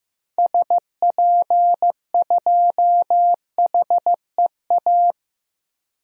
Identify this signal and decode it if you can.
Morse code, heard as a keyed tone, about 16 wpm, 700 Hz. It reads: SP2HEA
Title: DXCC-100 Call Signs - 6 Characters - 15wpm